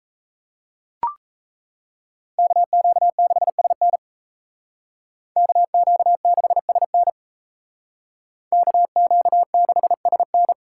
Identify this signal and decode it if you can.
Morse code, heard as a keyed tone, about 34 wpm, 700 Hz. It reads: KQ6SN KQ6SN KQ6SN
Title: Speed Racing - USA 2x2 Call Signs 28wpm